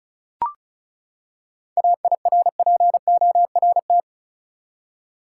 Morse code, heard as a keyed tone, about 36 wpm, 700 Hz. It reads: AIRPORT
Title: Seven Letter Words - 35wpm